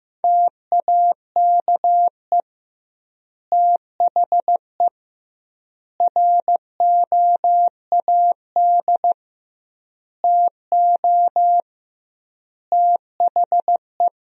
Morse code, heard as a keyed tone, about 15 wpm, 700 Hz. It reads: TAKE THE ROAD TO THE
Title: 2x Word-Spacing - Sentences from Top 400 Words - 15wpm